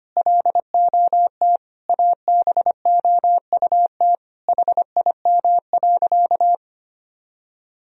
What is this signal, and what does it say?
Morse code, heard as a keyed tone, about 25 wpm, 700 Hz. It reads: LOT ABOUT HIM.
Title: Sentences from Top 1000 Words - Review Part 3 - 25wpm